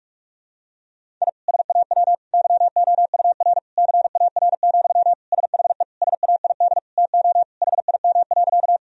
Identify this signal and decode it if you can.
Morse code, heard as a keyed tone, about 41 wpm, 700 Hz. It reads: I SAW YOUR CAR, SHE SAID TO HIM.
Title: Sentences from Top 1000 Words - Review Part 1 - 45wpm